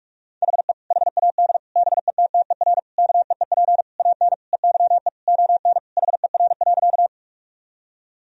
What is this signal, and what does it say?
Morse code, heard as a keyed tone, about 48 wpm, 700 Hz. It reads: HE HAD BETTER KEEP AN EYE ON HER.
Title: Sentences from Top 1000 Words - Review Part 1 - 45wpm